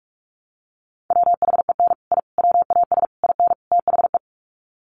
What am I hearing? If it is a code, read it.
Morse code, heard as a keyed tone, about 44 wpm, 700 Hz. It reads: WHEN I WAS IN THE
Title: Sets of 5 Words Encore2 45wpm